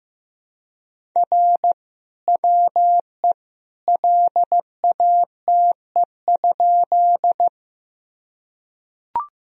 Morse code, heard as a keyed tone, about 15 wpm, 700 Hz. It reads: R WE LATE?
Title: Sentences from Top 300 Words and Abbreviations 15wpm